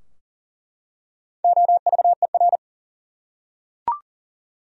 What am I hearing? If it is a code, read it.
Morse code, heard as a keyed tone, about 40 wpm, 700 Hz. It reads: OVER
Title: Top 100 Words 40wpm